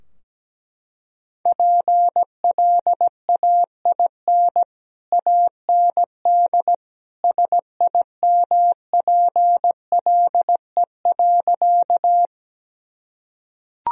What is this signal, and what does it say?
Morse code, heard as a keyed tone, about 17 wpm, 700 Hz. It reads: PLAIN AND SIMPLE.
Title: Sentences from Top 1000 Words - Review Part 2 - 17wpm